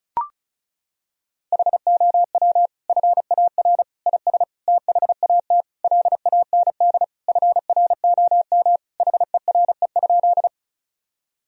Morse code, heard as a keyed tone, about 36 wpm, 700 Hz. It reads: HOW FAR IS THAT LAND FROM HERE?
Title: Sentences from Top 1000 Words - Review Part 1 - 35wpm